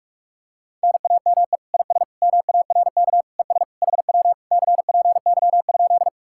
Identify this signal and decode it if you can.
Morse code, heard as a keyed tone, about 49 wpm, 700 Hz. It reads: NAME IS MARK ES HW CPY?